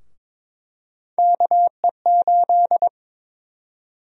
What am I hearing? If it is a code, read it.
Morse code, heard as a keyed tone, about 22 wpm, 700 Hz. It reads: KE8
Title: USA Call Sign Prefixes - Repeated 2x - 22wpm